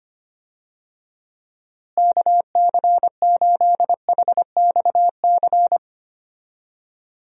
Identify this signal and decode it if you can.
Morse code, heard as a keyed tone, about 25 wpm, 700 Hz. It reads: KC8HXC
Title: USA Call Signs - 6 Characters - Repeated 2x - 25wpm